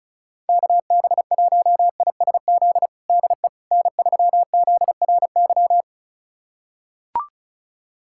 Morse code, heard as a keyed tone, about 36 wpm, 700 Hz. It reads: KB1ISZ DE N3ZRY
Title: USA Callsign DE Callsign 35wpm